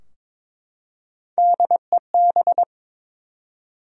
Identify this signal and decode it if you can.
Morse code, heard as a keyed tone, about 22 wpm, 700 Hz. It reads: DEB
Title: USA Common Names - Max 3 Letters - 22wpm